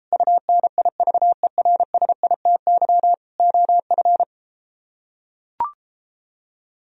Morse code, heard as a keyed tone, about 34 wpm, 700 Hz. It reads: UNIVERSITY OF